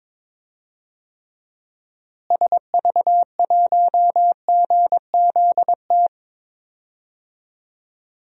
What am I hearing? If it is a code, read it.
Morse code, heard as a keyed tone, about 22 wpm, 700 Hz. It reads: SV1GZT
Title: DXCC-100 Call Signs - 6 Characters - Repeated 3x - 22wpm